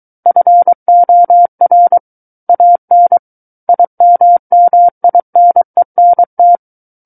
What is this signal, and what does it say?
Morse code, heard as a keyed tone, about 23 wpm, 700 Hz. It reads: FOR AN IMMINENT
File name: Book One / 14-Chapter 14 - 23 WPM